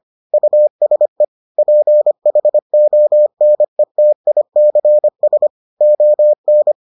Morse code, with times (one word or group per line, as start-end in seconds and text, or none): 0.3-1.2 USE
1.6-5.5 PHONETICS
5.8-6.7 ON